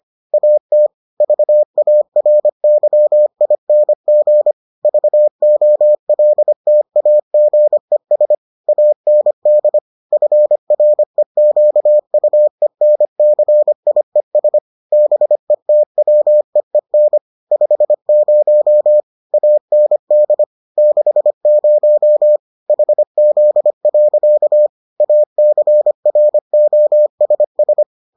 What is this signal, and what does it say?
Morse code, heard as a keyed tone, about 25 wpm, 600 Hz. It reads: AT VARYING VOLTAGES AND FREQUENCIES BETWEEN 50 AND 60 HZ. ACROSS